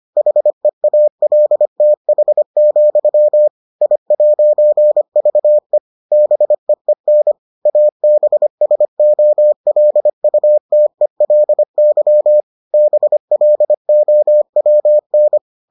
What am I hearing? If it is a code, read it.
Morse code, heard as a keyed tone, about 25 wpm, 600 Hz. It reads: HEALTH, I'VE BEEN ABSOLUTELY BLOWN